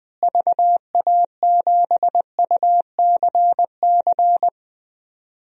Morse code, heard as a keyed tone, about 20 wpm, 700 Hz. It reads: VA7UCC
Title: DXCC-100 Call Signs - 6 Characters - Repeated 2x - 20wpm